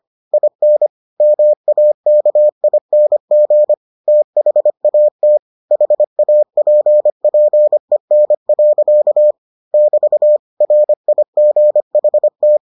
Morse code, heard as a keyed tone, about 25 wpm, 600 Hz. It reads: IN MAKING THAT HAPPEN. <BT> RIGHT